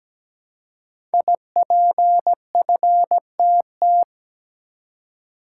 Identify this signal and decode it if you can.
Morse code, heard as a keyed tone, about 17 wpm, 700 Hz. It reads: IPFTT
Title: Code Groups - 5 Letters - 17wpm